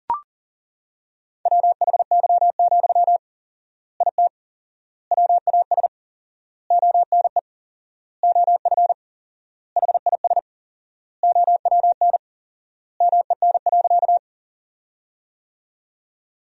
Morse code, heard as a keyed tone, about 40 wpm, 700 Hz. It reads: WHY, IT WAS ONE OF HIS OWN MEN.
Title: 4x Word-Spacing - Sentences from Top 300 Words - 40wpm